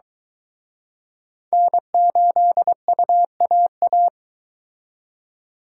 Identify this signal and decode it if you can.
Morse code, heard as a keyed tone, about 23 wpm, 700 Hz. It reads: N8UAA